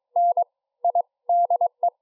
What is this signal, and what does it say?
Morse code, heard as a keyed tone, about 22 wpm, 700 Hz. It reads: N I DE
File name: iw0ffk / 2008 07 11 IT9DLN IW0FFK